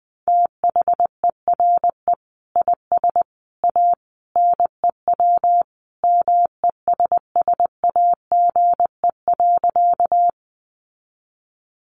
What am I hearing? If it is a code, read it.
Morse code, heard as a keyed tone, about 20 wpm, 700 Hz. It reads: THERE IS A NEW MESSAGE.
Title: Sentences from Top 1100 Words 20wpm